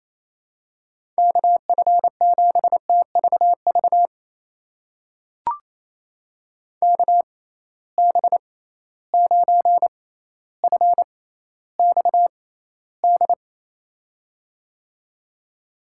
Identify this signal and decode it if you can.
Morse code, heard as a keyed tone, about 28 wpm, spaced as for 14 wpm, 700 Hz. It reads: KF7TVV KB9FXD
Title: USA Call Signs - 6 Characters - 6x Character Spacing - 28wpm